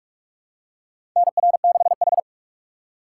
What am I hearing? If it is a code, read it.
Morse code, heard as a keyed tone, about 47 wpm, 700 Hz. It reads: NR6H
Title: USA Call Signs - 4 Characters - 6x Character Spacing - 45wpm